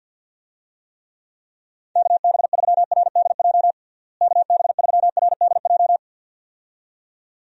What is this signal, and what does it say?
Morse code, heard as a keyed tone, about 52 wpm, 700 Hz. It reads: KB3RDJ KB3RDJ
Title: USA Call Signs - 6 Characters - Repeated 2x - 50wpm